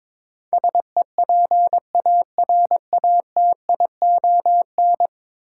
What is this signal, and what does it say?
Morse code, heard as a keyed tone, about 22 wpm, 700 Hz. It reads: SEPARATION
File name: practice / 22wpm / 10 Letter Words - First 3 Letters Given - 22wpm